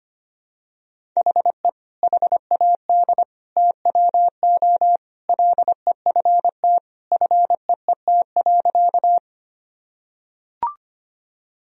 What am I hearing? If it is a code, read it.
Morse code, heard as a keyed tone, about 25 wpm, 700 Hz. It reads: HE HAD TWO LEFT FEET.
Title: Sentences from Top 1000 Words - Review Part 1 - 25wpm